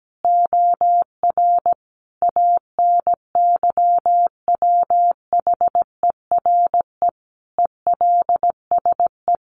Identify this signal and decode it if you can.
Morse code, heard as a keyed tone, about 17 wpm, 700 Hz. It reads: OR ANYWHERE ELSE
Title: Sentences from Top 1400 Words 17wpm